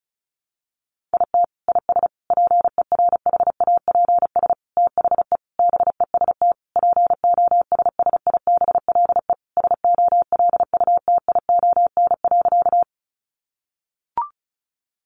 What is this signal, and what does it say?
Morse code, heard as a keyed tone, about 34 wpm, 700 Hz. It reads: IT IS PERHAPS THE BEST POSSIBLE SOLUTION.
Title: Sentences from Top 1300 Words 35wpm